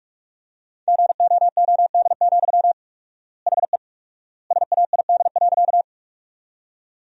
Morse code, heard as a keyed tone, about 45 wpm, 700 Hz. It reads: GOOD, HE SAID.
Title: 4x Word-Spacing - Sentences from Top 200 Words - 45wpm